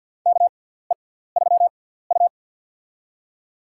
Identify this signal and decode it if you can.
Morse code, heard as a keyed tone, about 49 wpm, spaced as for 17 wpm, 700 Hz. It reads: KE3U
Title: USA Call Signs - 4 Characters - 6x Character Spacing - 50wpm